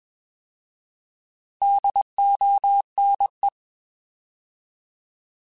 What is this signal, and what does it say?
Morse code, heard as a keyed tone, about 21 wpm, 800 Hz. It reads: DONE
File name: Alphabetical / Alphabetical mp3s / Ds @20